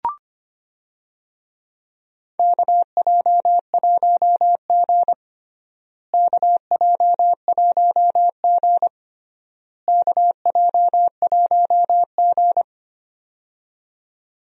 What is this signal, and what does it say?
Morse code, heard as a keyed tone, about 25 wpm, 700 Hz. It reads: KJ1G KJ1G KJ1G
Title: USA Call Signs - 4 Characters - Repeated 3x - 25wpm